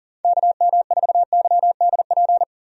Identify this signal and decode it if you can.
Morse code, heard as a keyed tone, about 40 wpm, 700 Hz. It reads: KM4YDP
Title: USA Call Signs - 6 Characters - 40wpm